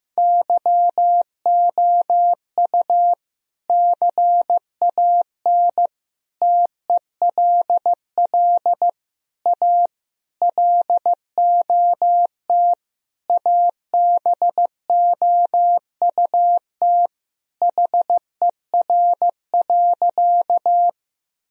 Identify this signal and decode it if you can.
Morse code, heard as a keyed tone, about 15 wpm, 700 Hz. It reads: YOU CAN TELL A LOT ABOUT HER.
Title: Sentences from Top 1000 Words - Review Part 3 - 15wpm